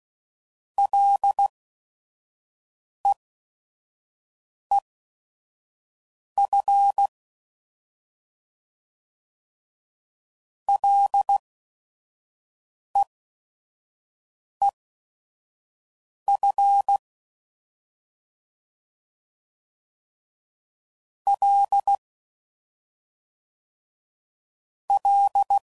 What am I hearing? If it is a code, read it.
Morse code, heard as a keyed tone, about 16 wpm, spaced as for 5 wpm, 800 Hz. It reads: LEEF LEEF L L
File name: Prefixes/FLs & THs/Single L's